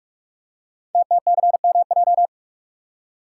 Morse code, heard as a keyed tone, about 44 wpm, 690 Hz. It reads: TTCMJ
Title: Code Groups - 5 Letters - 45wpm